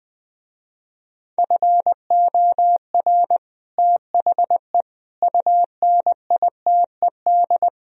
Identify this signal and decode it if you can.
Morse code, heard as a keyed tone, about 20 wpm, 700 Hz. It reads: FOR THE UNITED
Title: Speed-Racing - Sets of 4 Words 17wpm